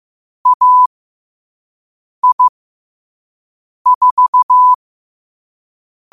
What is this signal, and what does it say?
Morse code, heard as a keyed tone, about 15 wpm, spaced as for 5 wpm, 1000 Hz. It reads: AI4
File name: qotdfiles / 5wpm